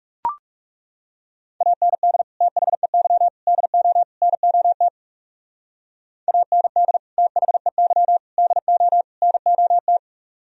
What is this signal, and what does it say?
Morse code, heard as a keyed tone, about 36 wpm, 700 Hz. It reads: AND THEY DO NOT AND THEY DO NOT
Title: Speed-Racing - Sets of 4 Words 30wpm